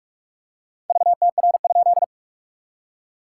Morse code, heard as a keyed tone, about 41 wpm, 690 Hz. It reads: VTR?
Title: SOTA QSO Practice 45wpm